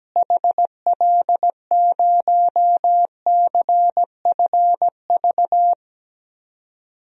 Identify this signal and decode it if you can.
Morse code, heard as a keyed tone, about 17 wpm, 700 Hz. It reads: HL0CFV
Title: DXCC-100 Call Signs - 6 Characters - Repeated 2x - 17wpm